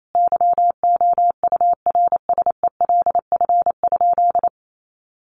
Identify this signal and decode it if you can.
Morse code, heard as a keyed tone, about 28 wpm, 700 Hz. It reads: YOURSELF?
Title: Sentences from Top 1300 Words 28wpm